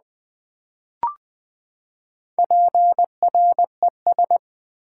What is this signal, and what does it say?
Morse code, heard as a keyed tone, about 20 wpm, 700 Hz. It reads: PRES
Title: Build A Word - 4 Letters At A Time - 10-Letter Words - 20wpm